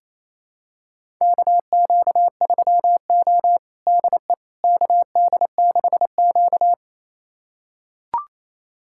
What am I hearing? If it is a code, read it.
Morse code, heard as a keyed tone, about 28 wpm, 700 Hz. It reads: KQ3O DE KD6Q
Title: USA Callsign DE Callsign - 4 Characters - 28wpm